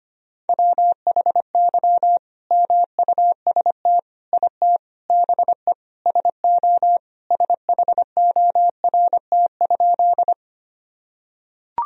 Morse code, heard as a keyed tone, about 25 wpm, 700 Hz. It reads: WHY MUST IT BE SO SHORT?
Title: Sentences from Top 1000 Words - Review Part 2 - 25wpm